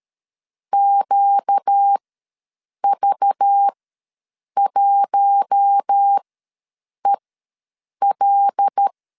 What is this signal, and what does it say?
Morse code, heard as a keyed tone, about 13 wpm, spaced as for 7 wpm, 800 Hz. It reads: QV1EL
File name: LEZIONI / 6 PKX0 / 6 PK ws 12wpm 3116